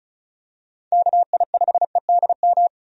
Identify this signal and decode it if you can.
Morse code, heard as a keyed tone, about 36 wpm, 700 Hz. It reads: KI5EDM
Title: USA Call Signs - 6 Characters - 35wpm